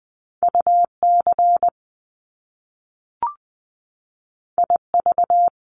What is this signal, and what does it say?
Morse code, heard as a keyed tone, about 20 wpm, 700 Hz. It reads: UC IV
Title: Top Two Letter Combinations in Words 20wpm